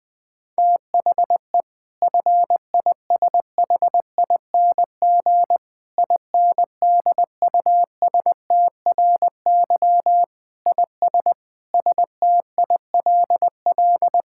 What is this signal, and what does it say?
Morse code, heard as a keyed tone, about 20 wpm, 700 Hz. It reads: THE FISHING INDUSTRY IS STILL
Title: Sentences from Top 1000 Words - Review Part 6 - 20wpm